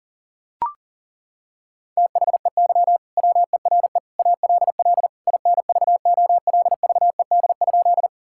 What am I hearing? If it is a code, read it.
Morse code, heard as a keyed tone, about 40 wpm, 700 Hz. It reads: THEY WERE ALL INVOLVED?
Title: Sentences from Top 1000 Words - Review Part 5 - 40wpm